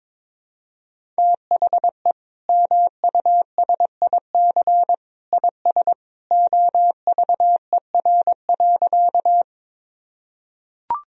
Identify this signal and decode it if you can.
Morse code, heard as a keyed tone, about 22 wpm, 700 Hz. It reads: THE MUSIC IS OVER.
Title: Sentences from Top 1000 Words - Review Part 1 - 22wpm